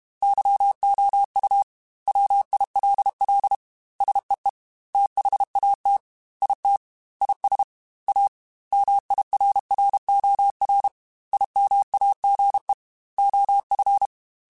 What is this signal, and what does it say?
Morse code, heard as a keyed tone, about 32 wpm, 800 Hz. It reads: YOU WILL SEE THAT IT IS A MIRROR IMAGE OF
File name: Technical Subjects / Technical Subjects-mp3s / OR Gate @30